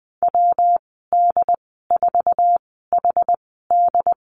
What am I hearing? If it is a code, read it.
Morse code, heard as a keyed tone, about 20 wpm, spaced as for 15 wpm, 700 Hz. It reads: WD4HD
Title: 128 Lesson 21g - Number 6 - Call Signs 20wpm